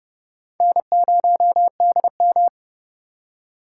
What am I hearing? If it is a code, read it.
Morse code, heard as a keyed tone, about 38 wpm, 700 Hz. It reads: N0DM